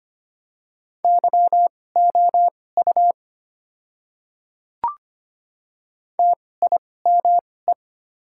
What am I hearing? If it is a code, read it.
Morse code, heard as a keyed tone, about 25 wpm, spaced as for 18 wpm, 700 Hz. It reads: YOU TIME
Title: 2x Character Spacing - Top 100 Words - 25wpm